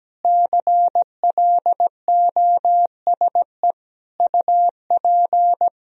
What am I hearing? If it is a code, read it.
Morse code, heard as a keyed tone, about 17 wpm, 700 Hz. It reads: CLOSE UP
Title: Sentences from Top 1000 Words - Review Part 1 - 17wpm